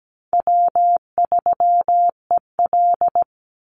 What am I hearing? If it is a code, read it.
Morse code, heard as a keyed tone, about 17 wpm, 700 Hz. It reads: W3EL
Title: Easy Call Signs - 17wpm